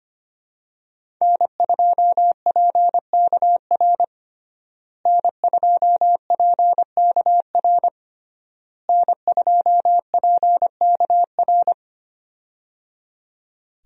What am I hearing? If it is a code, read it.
Morse code, heard as a keyed tone, about 25 wpm, 700 Hz. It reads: N2PKR N2PKR N2PKR
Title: USA Call Signs - 5 Characters - Repeated 3x - 25wpm